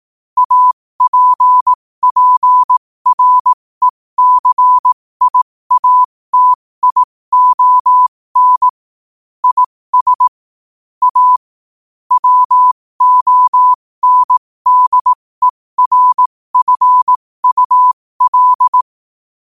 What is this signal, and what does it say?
Morse code, heard as a keyed tone, about 18 wpm, spaced as for 15 wpm, 1000 Hz. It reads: APPRECIATION IS A WONDERFUL
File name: qotdfiles / 15wpm